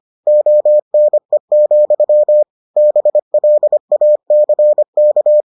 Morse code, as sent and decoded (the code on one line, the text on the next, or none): --- -. --..-- / -... .-.. .- -.-. -.-
ONE, BLACK